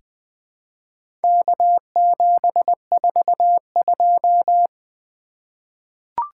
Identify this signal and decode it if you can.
Morse code, heard as a keyed tone, about 20 wpm, 700 Hz. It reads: K742
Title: Four Letter-Number Combinations - 20wpm